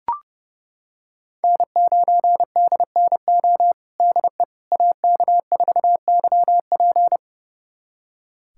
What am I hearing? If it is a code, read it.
Morse code, heard as a keyed tone, about 32 wpm, 700 Hz. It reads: N9DNO DE AK4YP
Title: USA Callsign DE Callsign - 5 Characters - 30wpm